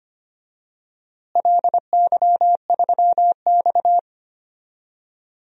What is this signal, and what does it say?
Morse code, heard as a keyed tone, about 25 wpm, 700 Hz. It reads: LY3X
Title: DXCC-100 Call Signs - 4 Characters - Repeated 3x - 25wpm